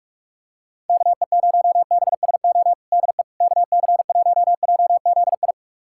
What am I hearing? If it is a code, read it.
Morse code, heard as a keyed tone, about 47 wpm, 700 Hz. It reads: KE0BSO DE KC1JZI
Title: USA Callsign DE Callsign - 6 Characters - 45wpm